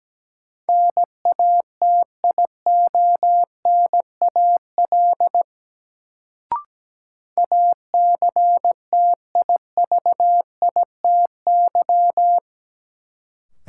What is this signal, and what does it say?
Morse code, heard as a keyed tone, about 17 wpm, 700 Hz. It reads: NATIONAL ACTIVITY
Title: Sentences from Top 1000 Words - Review Part 4 - New Word Drill - 17wpm